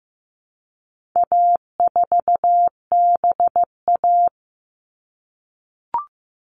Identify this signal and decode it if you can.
Morse code, heard as a keyed tone, about 15 wpm, 700 Hz. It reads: A4BA
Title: All Cut Numbers - 4-Digit 15wpm